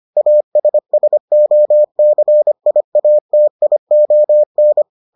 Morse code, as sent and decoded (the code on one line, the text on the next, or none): .- ... ... --- -.-. .. .- - .. --- -.
ASSOCIATION